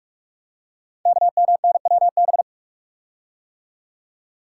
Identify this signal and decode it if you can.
Morse code, heard as a keyed tone, about 44 wpm, 700 Hz. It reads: KMNWB
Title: Code Groups - 5 Letters - 45wpm